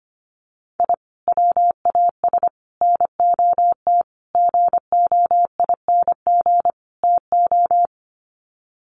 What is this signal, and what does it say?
Morse code, heard as a keyed tone, about 25 wpm, 700 Hz. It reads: I WAS NOT GOING TO
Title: Sets of 5 Words 25wpm